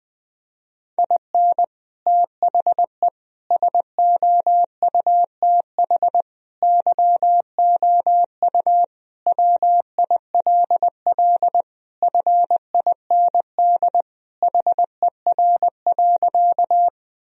Morse code, heard as a keyed tone, about 20 wpm, 700 Hz. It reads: IN THE SOUTH YOU WILL FIND HER.
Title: Sentences from Top 1000 Words - Review Part 2 - 20wpm